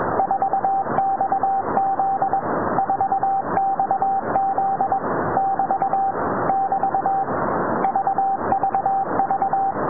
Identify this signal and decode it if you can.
Morse code, heard as a keyed tone, about 21 wpm, 780 Hz, with noise. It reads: 4XZ 4XZ <BT> <BT> VVV